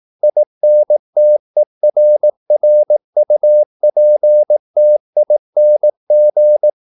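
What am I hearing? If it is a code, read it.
Morse code, heard as a keyed tone, about 18 wpm, 600 Hz. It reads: INTERRUPTING
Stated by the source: Morse practice at 18 wpm, English text, from CW med Gnister.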